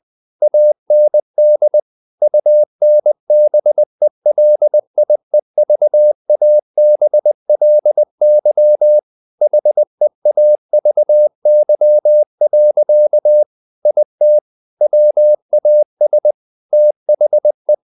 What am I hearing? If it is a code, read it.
Morse code, heard as a keyed tone, about 20 wpm, 600 Hz. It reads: AND UNBELIEVABLY HEAVY. IT WAS THE
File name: the old man and the sea / the old man and the sea part 1 20wpm